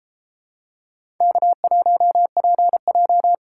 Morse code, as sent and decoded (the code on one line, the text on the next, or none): -.- .---- .--. .---
K1PJ